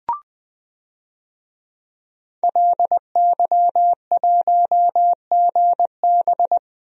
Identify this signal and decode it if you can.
Morse code, heard as a keyed tone, about 21 wpm, 700 Hz. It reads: LY1GB
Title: DXCC-100 Call Signs - 5 Characters - Repeated 3x - 20wpm